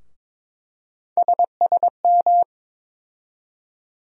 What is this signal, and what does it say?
Morse code, heard as a keyed tone, about 21 wpm, 700 Hz. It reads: SSM
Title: USA Call Sign Postfixes - 3 Letters 22wpm